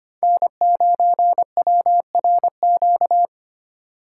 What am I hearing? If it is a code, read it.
Morse code, heard as a keyed tone, about 25 wpm, 700 Hz. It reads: N9WRQ
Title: USA Call Signs - 5 Characters - 4x Character Spacing - 25wpm